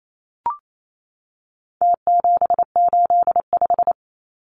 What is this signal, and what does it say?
Morse code, heard as a keyed tone, about 28 wpm, 700 Hz. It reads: T785
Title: Numbers - Contest Style - 4-Digit 28wpm